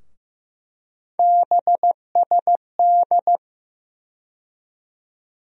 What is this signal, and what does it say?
Morse code, heard as a keyed tone, about 15 wpm, 700 Hz. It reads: BSD
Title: USA Call Sign Postfixes - 3 Letters 15wpm